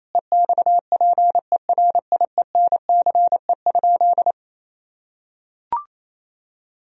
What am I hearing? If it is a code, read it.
Morse code, heard as a keyed tone, about 28 wpm, 700 Hz. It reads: EXPERIENCE?
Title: Sentences from Top 1000 Words - Review Part 3 - 28wpm